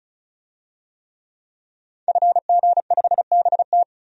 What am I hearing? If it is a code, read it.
Morse code, heard as a keyed tone, about 34 wpm, 700 Hz. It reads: FG5BT
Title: DXCC-100 Call Signs - 5 Characters - 35wpm